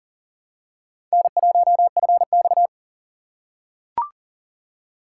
Morse code, heard as a keyed tone, about 40 wpm, 700 Hz. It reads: N1FX
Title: USA Call Signs - 4 Characters - 4x Character Spacing - 40wpm